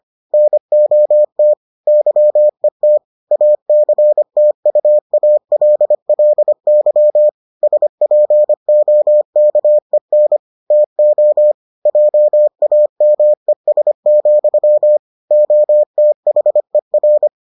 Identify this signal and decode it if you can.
Morse code, heard as a keyed tone, about 25 wpm, 600 Hz. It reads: NOT YET ACTUALLY SPOKEN TO JAMES, OTHER